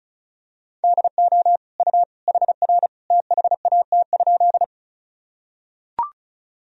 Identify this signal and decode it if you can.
Morse code, heard as a keyed tone, about 36 wpm, 700 Hz. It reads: DO U HR THAT?
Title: Sentences from Top 300 Words and Abbreviations 35wpm